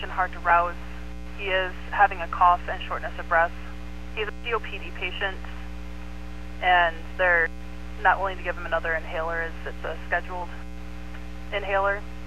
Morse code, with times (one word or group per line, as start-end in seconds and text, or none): none